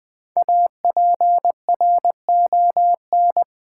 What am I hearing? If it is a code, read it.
Morse code, heard as a keyed tone, about 20 wpm, 700 Hz. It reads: APRON
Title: Easy Words with Hints - 20wpm